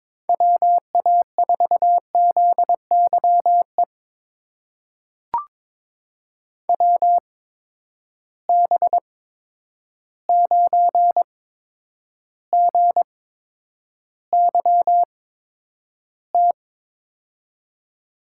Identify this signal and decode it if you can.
Morse code, heard as a keyed tone, about 22 wpm, spaced as for 9 wpm, 700 Hz. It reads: WA4ZYE WB9GYT
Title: USA Call Signs - 6 Characters - 8x Character Spacing - 22wpm